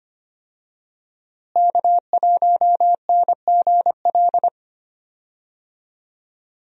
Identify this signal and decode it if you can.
Morse code, heard as a keyed tone, about 25 wpm, 700 Hz. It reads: K1NGL
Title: USA Call Signs - 5 Characters - Repeated 3x - 25wpm